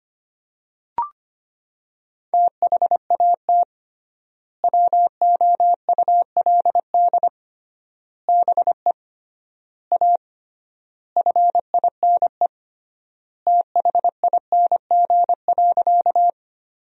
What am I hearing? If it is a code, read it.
Morse code, heard as a keyed tone, about 25 wpm, 700 Hz. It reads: THAT WOULD BE A FINE THING.
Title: 3x Word-Spacing - Sentences from Top 500 Words - 25wpm